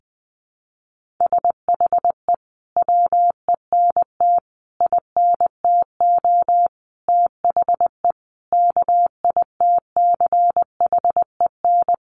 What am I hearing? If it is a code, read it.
Morse code, heard as a keyed tone, about 20 wpm, 700 Hz. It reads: SHE WENT INTO THE KITCHEN
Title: Sentences from Top 1400 Words 20wpm